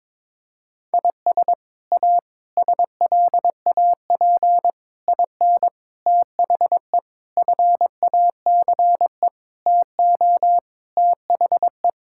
Morse code, heard as a keyed tone, about 22 wpm, 700 Hz. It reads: IS A SLAP IN THE FACE TO THE
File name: practice / 22wpm / Sets of 8 Words - 22wpm